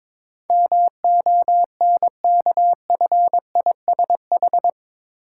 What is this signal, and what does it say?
Morse code, heard as a keyed tone, about 22 wpm, 690 Hz. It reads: MONKFISH
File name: Difficult Words with Hints - 22wpm